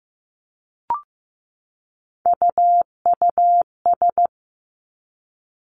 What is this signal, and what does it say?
Morse code, heard as a keyed tone, about 15 wpm, 700 Hz. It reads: UUS
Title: Three Challenging Letter-Number Combinations 15wpm